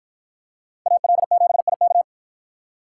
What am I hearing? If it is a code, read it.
Morse code, heard as a keyed tone, about 50 wpm, 700 Hz. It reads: AL7IK